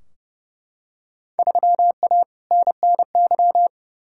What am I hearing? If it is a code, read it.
Morse code, heard as a keyed tone, about 30 wpm, 700 Hz. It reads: 3A NNY
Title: Field Day Exchanges - 30wpm